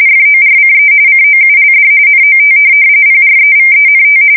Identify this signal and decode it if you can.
RTTY, about 45.45 baud. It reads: FORCE COULD NOT PREVENT
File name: fsk170 / USS-New-Jersey